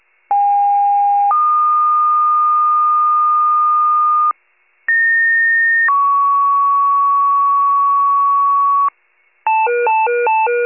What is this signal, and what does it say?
Two-tone page, heard as a one-second tone, then a three-second tone, about 810 then 1220 Hz, about 1850 then 1120 Hz. It is heard from 0.3 to 8.9 s.
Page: County Wide Paging Archive